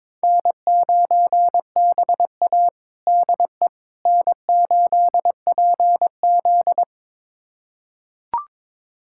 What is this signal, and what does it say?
Morse code, heard as a keyed tone, about 22 wpm, 700 Hz. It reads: N9BA DE N8PZ
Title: USA Callsign DE Callsign - 4 Characters - 22wpm